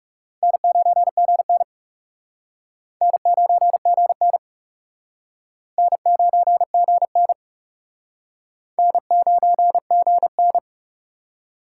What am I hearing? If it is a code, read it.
Morse code, heard as a keyed tone, about 38 wpm, 700 Hz. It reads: N9GN N9GN N9GN N9GN
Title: Speed Racing - USA 1x2 Call Signs 30wpm